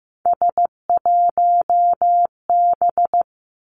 Morse code, heard as a keyed tone, about 15 wpm, 700 Hz. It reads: S1B
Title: Three Challenging Letter-Number Combinations 15wpm